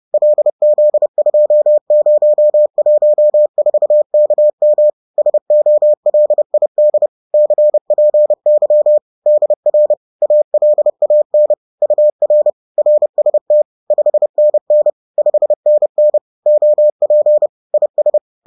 Morse code, heard as a keyed tone, about 30 wpm, 600 Hz. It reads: LZ2014KM SOLID CPY DR ALAN UR RST 5NN 5NN OP IS
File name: 30wpm-7